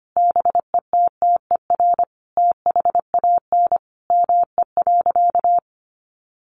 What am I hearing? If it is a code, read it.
Morse code, heard as a keyed tone, about 25 wpm, 700 Hz. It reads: BETTER THAN ME.
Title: Sentences from Top 400 Words 25wpm